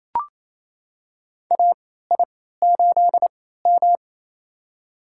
Morse code, heard as a keyed tone, about 28 wpm, spaced as for 16 wpm, 700 Hz. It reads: AI8M
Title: USA Call Signs - 3x Character Spacing - 28wpm